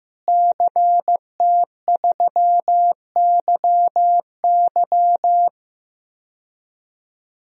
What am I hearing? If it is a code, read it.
Morse code, heard as a keyed tone, about 15 wpm, 700 Hz. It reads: CT3YY
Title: DXCC-100 Call Signs - 5 Characters - Repeated 3x - 15wpm